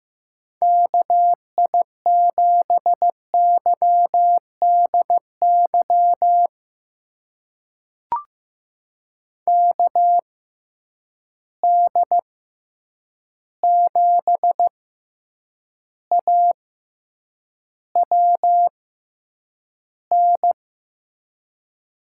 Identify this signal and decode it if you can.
Morse code, heard as a keyed tone, about 15 wpm, spaced as for 8 wpm, 700 Hz. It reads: KI7YDY KD7AWN
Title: USA Call Signs - 6 Characters - 6x Character Spacing - 15wpm